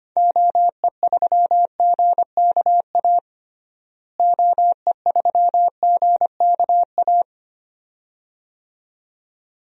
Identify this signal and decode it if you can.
Morse code, heard as a keyed tone, about 25 wpm, 700 Hz. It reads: OE3GKA OE3GKA
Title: DXCC-100 Call Signs - 6 Characters - Repeated 2x - 25wpm